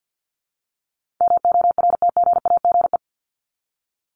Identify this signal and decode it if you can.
Morse code, heard as a keyed tone, about 50 wpm, 700 Hz. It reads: MORTGAGE
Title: Top 1751-2000 Words 50wpm